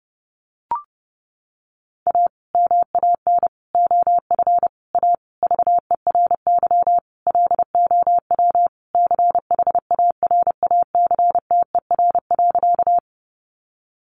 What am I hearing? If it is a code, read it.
Morse code, heard as a keyed tone, about 30 wpm, 700 Hz. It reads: A MAN OF A VERY LOW CHARACTER.
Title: Sentences from Top 1200 Words 30wpm